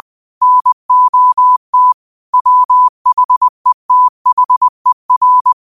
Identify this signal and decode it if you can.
Morse code, heard as a keyed tone, about 20 wpm, 1000 Hz. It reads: NOT WHETHER
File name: 2024-09-13-20wpm